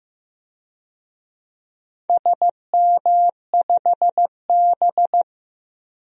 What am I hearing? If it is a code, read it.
Morse code, heard as a keyed tone, about 15 wpm, 700 Hz. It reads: SM5B
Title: DXCC-100 Call Signs - 4 Characters - 15wpm